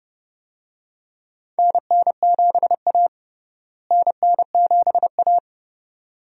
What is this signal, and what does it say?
Morse code, heard as a keyed tone, about 30 wpm, 700 Hz. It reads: NN7A NN7A
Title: USA Call Signs - 4 Characters - Repeated 2x - 30wpm